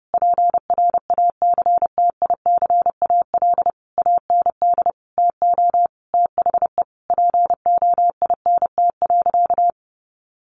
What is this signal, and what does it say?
Morse code, heard as a keyed tone, about 30 wpm, 700 Hz. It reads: PRACTICAL AND TO THE POINT.
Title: Sentences from Top 1400 Words 30wpm